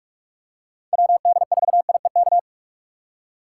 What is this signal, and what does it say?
Morse code, heard as a keyed tone, about 46 wpm, 700 Hz. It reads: WD4IEK
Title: USA Call Signs - 4x Character Spacing - 45wpm